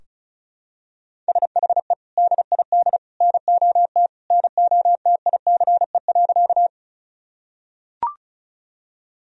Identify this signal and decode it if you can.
Morse code, heard as a keyed tone, about 34 wpm, 700 Hz. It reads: SHE DID NOT NOTICE.
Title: Sentences from Top 1000 Words - Review Part 2 - 35wpm